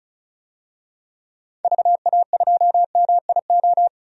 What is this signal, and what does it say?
Morse code, heard as a keyed tone, about 36 wpm, 700 Hz. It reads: VA2MIO
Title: DXCC-100 Call Signs - 6 Characters - 35wpm